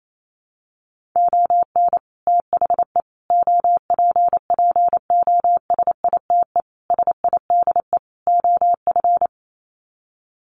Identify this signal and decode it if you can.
Morse code, heard as a keyed tone, about 28 wpm, 700 Hz. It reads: ON THE OPPOSITE SIDE OF
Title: Sets of 5 Words Encore2 28wpm